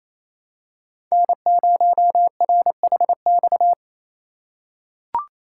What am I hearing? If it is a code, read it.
Morse code, heard as a keyed tone, about 28 wpm, 700 Hz. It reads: N0RHX
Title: USA Call Signs - 5 Characters - 2x Character Spacing - 28wpm